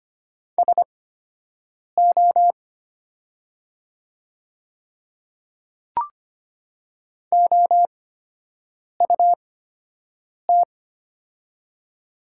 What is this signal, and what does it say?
Morse code, heard as a keyed tone, about 25 wpm, spaced as for 7 wpm, 700 Hz. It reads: SO OUT
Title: 8x Character Spacing - Top 100 Words - 25wpm